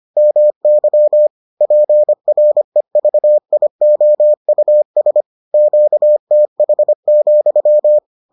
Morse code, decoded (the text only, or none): MY PREVIOUS QTH,